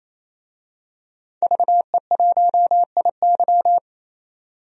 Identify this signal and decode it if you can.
Morse code, heard as a keyed tone, about 28 wpm, 700 Hz. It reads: VE1IY
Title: DXCC-100 Call Signs - 5 Characters - 28wpm